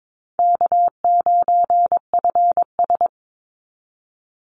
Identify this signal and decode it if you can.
Morse code, heard as a keyed tone, about 22 wpm, 700 Hz. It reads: K9FS
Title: Easy Call Signs Repeated 2x - 22wpm